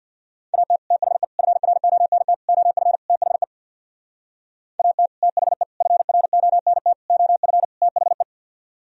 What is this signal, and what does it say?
Morse code, heard as a keyed tone, about 64 wpm, 700 Hz. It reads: AT THE FRONT OF THE AT THE FRONT OF THE